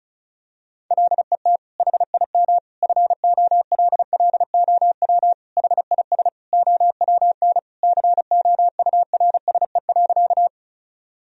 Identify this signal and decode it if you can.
Morse code, heard as a keyed tone, about 36 wpm, 700 Hz. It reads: LET HIM FOLLOW HIS OWN COURSE.
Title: Sentences from Top 1000 Words - Review Part 1 - 35wpm